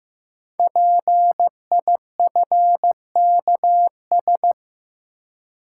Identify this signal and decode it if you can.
Morse code, heard as a keyed tone, about 15 wpm, 700 Hz. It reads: PIFKS
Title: Code Groups - 5 Letters - 15wpm